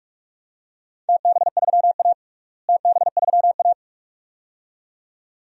Morse code, heard as a keyed tone, about 46 wpm, 700 Hz. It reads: TB3A TB3A
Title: DXCC-100 Call Signs - 4 Characters - Repeated 2x - 45wpm